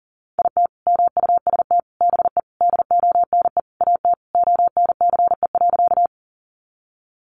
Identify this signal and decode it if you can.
Morse code, heard as a keyed tone, about 40 wpm, 700 Hz. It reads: IT MUST BE DONE AT ONCE.
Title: Sentences from Top 500 Words 40wpm